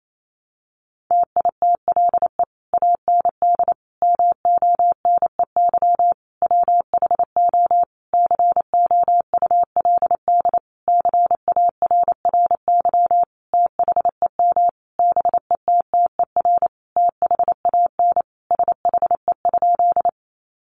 Morse code, heard as a keyed tone, about 28 wpm, 700 Hz. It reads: TITLE AND MONEY WHO COULD CARRY THEM BETTER THAN SHE?
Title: Sentences from Top 1200 Words 28wpm